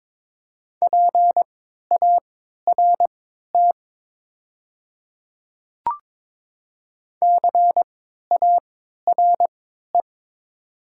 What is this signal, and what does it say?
Morse code, heard as a keyed tone, about 22 wpm, spaced as for 12 wpm, 700 Hz. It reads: PART CARE
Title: 3x Character Spacing - Top 500 Words - 22wpm